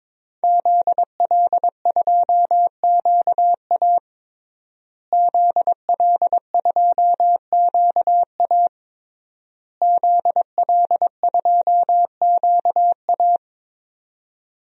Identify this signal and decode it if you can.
Morse code, heard as a keyed tone, about 22 wpm, 700 Hz. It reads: ZL2QA ZL2QA ZL2QA
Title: DXCC-100 Call Signs - 5 Characters - Repeated 3x - 22wpm